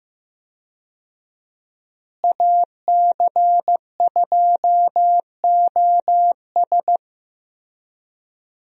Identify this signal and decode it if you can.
Morse code, heard as a keyed tone, about 15 wpm, 700 Hz. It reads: AC2OS
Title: USA Call Signs - 5 Characters - Repeated 2x - 15wpm